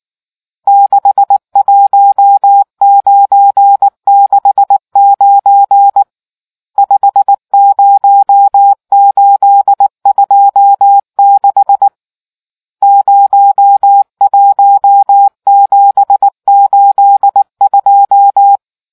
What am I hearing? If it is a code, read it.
Morse code, heard as a keyed tone, about 19 wpm, 800 Hz. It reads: 61969 50826 01782